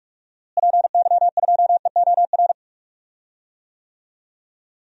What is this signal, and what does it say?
Morse code, heard as a keyed tone, about 47 wpm, 700 Hz. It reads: PY2EOR
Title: DXCC-100 Call Signs - 6 Characters - Repeated 2x - 45wpm